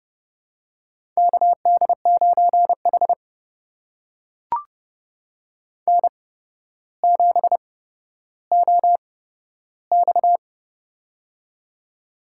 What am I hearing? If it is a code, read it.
Morse code, heard as a keyed tone, about 30 wpm, spaced as for 13 wpm, 700 Hz. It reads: KD9H N7OX
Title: USA Call Signs - 4 Characters - 8x Character Spacing - 30wpm